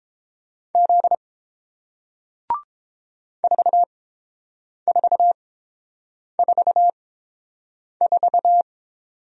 Morse code, heard as a keyed tone, about 32 wpm, spaced as for 8 wpm, 700 Hz. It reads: Z 4444